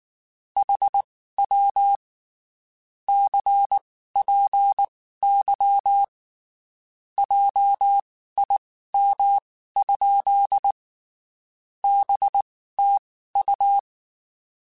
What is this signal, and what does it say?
Morse code, heard as a keyed tone, about 19 wpm, spaced as for 14 wpm, 800 Hz. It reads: HW CPY JIM? BTU
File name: (spaced) / QSOs Spaced-mp3s / WD0BGP 1 @18 sp